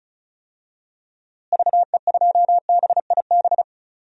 Morse code, heard as a keyed tone, about 34 wpm, 700 Hz. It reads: VE2BIB
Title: DXCC-100 Call Signs - 6 Characters - Repeated 2x - 35wpm